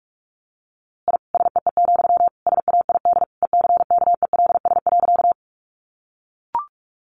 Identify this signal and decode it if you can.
Morse code, heard as a keyed tone, about 44 wpm, 700 Hz. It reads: I SEE, SAID ECKELS.
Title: A Sound Of Thunder 45wpm